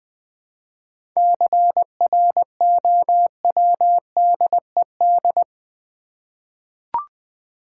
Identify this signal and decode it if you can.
Morse code, heard as a keyed tone, about 20 wpm, 700 Hz. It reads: CROWDED
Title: 7 Letter Words - First 3 Letters Given - 20wpm